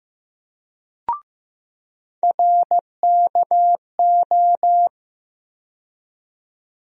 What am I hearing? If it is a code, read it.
Morse code, heard as a keyed tone, about 15 wpm, 700 Hz. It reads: RKO
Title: USA Call Sign Postfixes - 3 Letters 15wpm